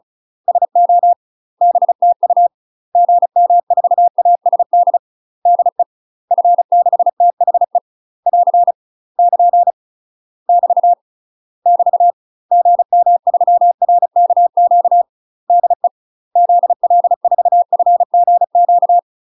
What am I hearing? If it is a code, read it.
Morse code, heard as a keyed tone, about 36 wpm, 700 Hz. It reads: SO BTU GM4ASD DE F6THE <AR> <KN> <BT> <BT> GM3RKQ DE ZL4FGQ
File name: UK MASTER QSOs TRAINING / 35WPM / UK MASTER QSOs 0017